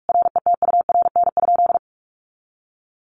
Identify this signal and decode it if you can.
Morse code, heard as a keyed tone, about 44 wpm, 700 Hz. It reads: RETURN?
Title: Sentences from Top 1300 Words 45wpm